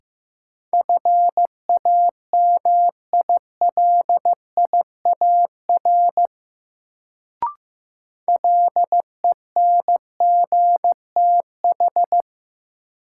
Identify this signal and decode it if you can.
Morse code, heard as a keyed tone, about 15 wpm, 700 Hz. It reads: FAMILIAR LENGTH
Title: Sentences from Top 1000 Words - New Word Drill - 15wpm